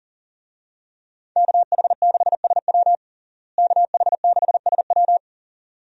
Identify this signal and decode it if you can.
Morse code, heard as a keyed tone, about 37 wpm, 700 Hz. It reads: KH6SW KH6SW
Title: DXCC-100 Call Signs - 5 Characters - Repeated 2x - 40wpm